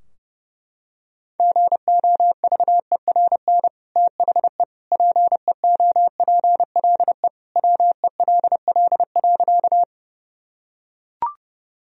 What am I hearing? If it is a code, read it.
Morse code, heard as a keyed tone, about 31 wpm, 700 Hz. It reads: GOVERN THE PEOPLE WELL.
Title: Sentences from Top 1000 Words - Review Part 2 - 30wpm